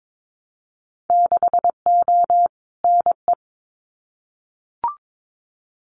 Morse code, heard as a keyed tone, about 22 wpm, 700 Hz. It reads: 6O NE
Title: Winter Field Day Exchanges Encore 22wpm